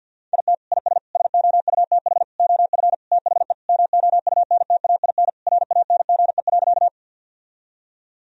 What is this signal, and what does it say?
Morse code, heard as a keyed tone, about 50 wpm, 700 Hz. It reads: IT IS SOUTH OF THE MOUNTAIN RANGE.
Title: Sentences from Top 1000 Words - Review Part 4 - 50wpm